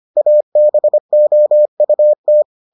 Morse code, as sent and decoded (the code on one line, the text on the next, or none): .- -... --- ..- -
ABOUT